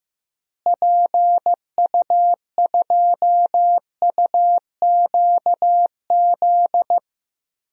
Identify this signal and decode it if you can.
Morse code, heard as a keyed tone, about 15 wpm, 700 Hz. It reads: PU2UQZ
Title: DXCC-100 Call Signs - 6 Characters - Repeated 3x - 15wpm